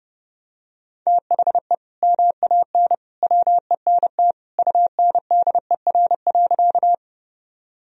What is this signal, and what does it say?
Morse code, heard as a keyed tone, about 32 wpm, 700 Hz. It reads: THE MAN WENT UNDER.
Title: Sentences from Top 1000 Words - Review Part 1 - 30wpm